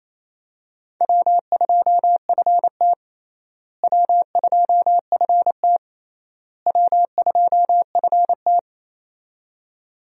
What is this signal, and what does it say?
Morse code, heard as a keyed tone, about 28 wpm, 700 Hz. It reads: W2FT W2FT W2FT
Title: USA Call Signs - 4 Characters - Repeated 3x - 28wpm